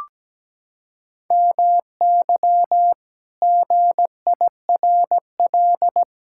Morse code, heard as a keyed tone, about 17 wpm, 700 Hz. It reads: MY GIRL
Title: Pangram Challenge v2 - 17wpm